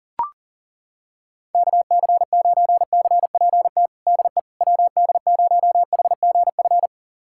Morse code, heard as a keyed tone, about 37 wpm, 700 Hz. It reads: KC9CPT DE WD0HGF
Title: USA Callsign DE Callsign - 6 Characters - 40wpm